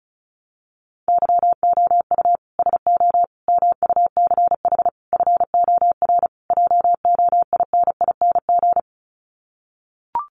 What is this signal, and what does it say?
Morse code, heard as a keyed tone, about 34 wpm, 700 Hz. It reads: YOU SO MUCH FOR JOINING
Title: Sets of 5 Words Encore2 35wpm